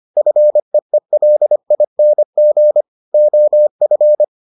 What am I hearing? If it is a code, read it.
Morse code, heard as a keyed tone, about 25 wpm, 600 Hz. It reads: FEELING OF